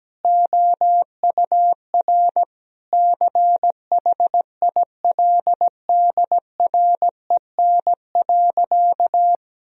Morse code, heard as a keyed tone, about 17 wpm, 700 Hz. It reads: OUR CHILDREN.
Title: Sentences from Top 1000 Words - Review Part 4 - 17wpm